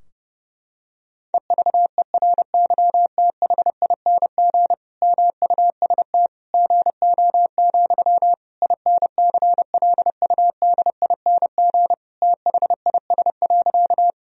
Morse code, heard as a keyed tone, about 30 wpm, 700 Hz. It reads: EVERYTHING MUST GO, INCLUDING THIS.
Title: Sentences from Top 1000 Words - Review Part 4 - 30wpm